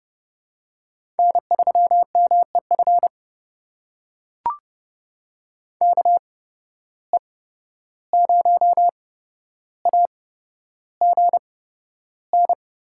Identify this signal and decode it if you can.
Morse code, heard as a keyed tone, about 30 wpm, spaced as for 12 wpm, 700 Hz. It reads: N3MEF KE0AGN
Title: USA Call Signs - 8x Character Spacing - 30wpm